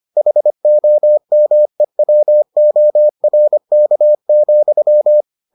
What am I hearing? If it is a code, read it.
Morse code, heard as a keyed tone, about 25 wpm, 600 Hz. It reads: HOMEWORK,